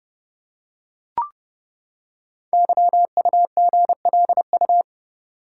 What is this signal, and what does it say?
Morse code, heard as a keyed tone, about 30 wpm, 700 Hz. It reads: YUGLU
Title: Code Groups - 5 Letter:Numbers - 30wpm